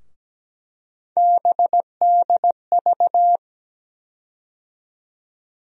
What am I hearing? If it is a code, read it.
Morse code, heard as a keyed tone, about 17 wpm, 700 Hz. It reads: BDV
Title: USA Call Sign Postfixes - 3 Letters 17wpm